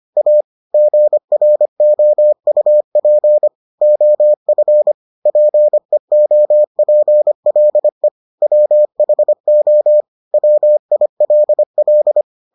Morse code, heard as a keyed tone, about 25 wpm, 600 Hz. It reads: A GROUP OF PEOPLE WHO WILL